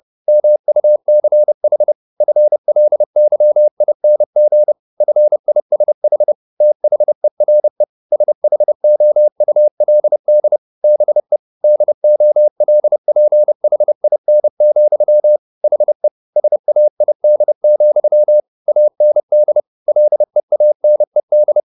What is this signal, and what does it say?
Morse code, heard as a keyed tone, about 30 wpm, 600 Hz. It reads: MUCH FLYING FISH THERE SHOULD BE DOLPHIN, HE SAID, AND LEANED